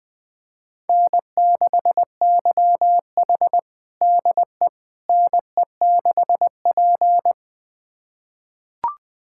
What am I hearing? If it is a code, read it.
Morse code, heard as a keyed tone, about 21 wpm, 700 Hz. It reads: N6YH DE NE6P
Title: USA Callsign DE Callsign - 4 Characters - 20wpm